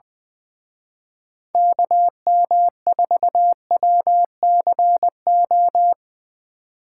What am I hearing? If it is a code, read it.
Morse code, heard as a keyed tone, about 23 wpm, 700 Hz. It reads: KM4WCO
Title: Speed Racing - USA 2x3 Call Signs 20wpm